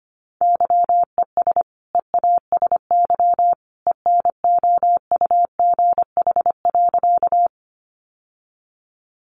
Morse code, heard as a keyed tone, about 25 wpm, 700 Hz. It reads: YES EASY ENOUGH.
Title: Sentences from Top 700 Words 25wpm